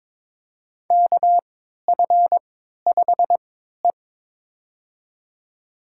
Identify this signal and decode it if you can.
Morse code, heard as a keyed tone, about 22 wpm, spaced as for 13 wpm, 700 Hz. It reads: KF5E
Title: USA Call Signs - 4 Characters - 3x Character Spacing - 22wpm